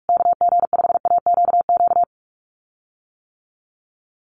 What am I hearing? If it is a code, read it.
Morse code, heard as a keyed tone, about 44 wpm, 690 Hz. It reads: KG5AQQ
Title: USA Call Signs Repeated 3x 45wpm